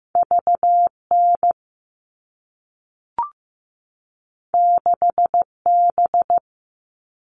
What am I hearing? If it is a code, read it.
Morse code, heard as a keyed tone, about 15 wpm, 700 Hz. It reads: VN 6B
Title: All Cut Numbers - 2-Digit 15wpm